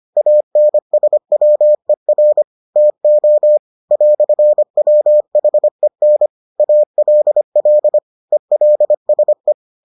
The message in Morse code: ANSWER TO "WHEN ALL ELSE

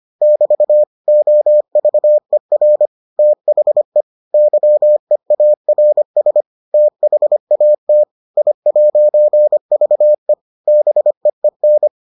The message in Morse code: <BT> OVER THE YEARS THAT I'VE BEEN

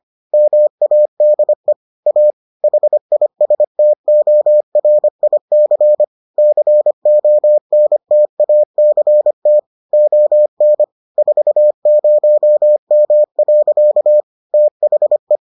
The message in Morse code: MADE A HISTORIC CONTACT ON 40M. THE

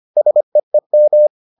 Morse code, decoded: SEEM

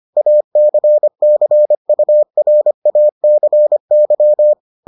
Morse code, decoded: ACCURACY

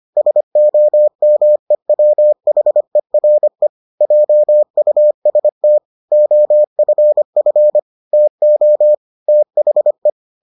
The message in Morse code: SOMEWHERE JUST OFF TO THE

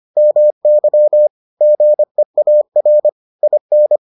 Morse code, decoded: MY GEAR IN

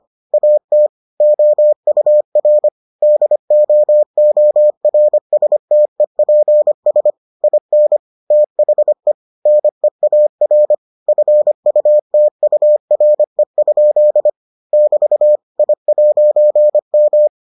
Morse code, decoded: AT OUR DOORSTEPS IN THE NEAR FUTURE? <BT> I'M

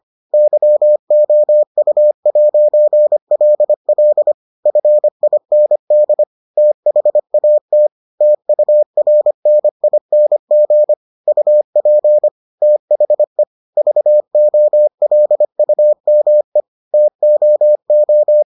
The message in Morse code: YOU'LL FIND THAT TURNING UP THE VOLUME TOO

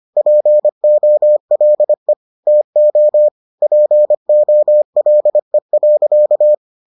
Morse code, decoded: POLE TO POLE.

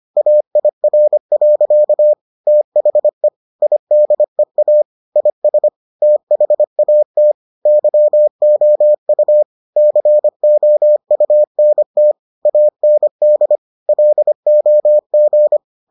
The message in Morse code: AIR. THE IDEA IS THAT YOU COUNT AND LOG

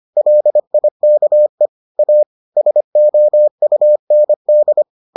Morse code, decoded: LIKE A SOUND

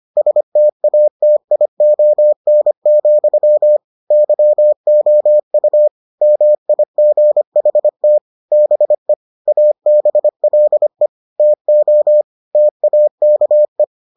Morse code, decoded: STATION, YOU MIGHT BE ABLE TO TAKE

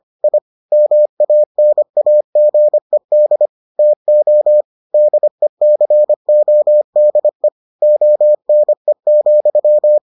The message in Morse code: I MANAGED TO DECODE ONE,

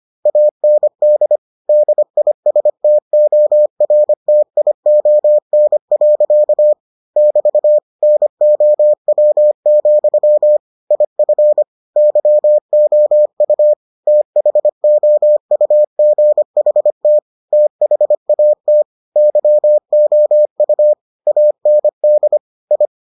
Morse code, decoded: AND DISTORTION. <BT> NOW, IF YOU THOUGHT THAT YOU AND I

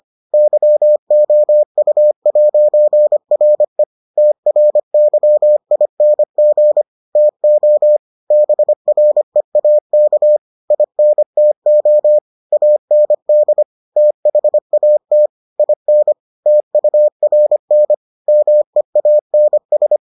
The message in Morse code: YOU'RE TRYING TO BREAK INTO AND THAT IN TURN MEANS

